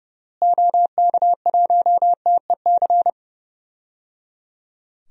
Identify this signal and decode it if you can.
Morse code, heard as a keyed tone, about 30 wpm, 700 Hz. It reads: OK1TEC
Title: DXCC-100 Call Signs - 6 Characters - Repeated 3x - 30wpm